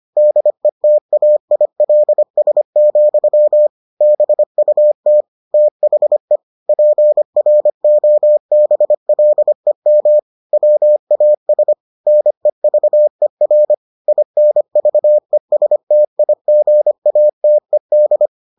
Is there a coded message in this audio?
Yes: Morse code from 0.2 to 18.3 s, heard as a keyed tone, about 25 wpm, 600 Hz. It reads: DETAILS, BUT THE PROBLEM WAS NEVER INVESTIGATED